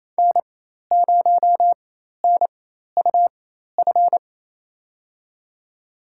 Morse code, heard as a keyed tone, about 28 wpm, spaced as for 13 wpm, 700 Hz. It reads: N0NUF
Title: USA Call Signs - 5 Characters - 4x Character Spacing - 28wpm